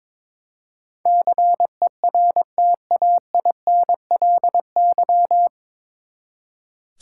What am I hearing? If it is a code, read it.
Morse code, heard as a keyed tone, about 22 wpm, 700 Hz. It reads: CERTAINLY
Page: Morse Code Ninja practice 22wpm